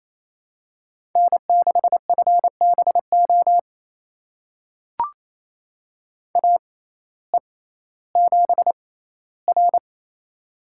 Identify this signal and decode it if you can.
Morse code, heard as a keyed tone, about 28 wpm, spaced as for 15 wpm, 700 Hz. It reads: N6FBO AE7R
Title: USA Call Signs - 6x Character Spacing - 28wpm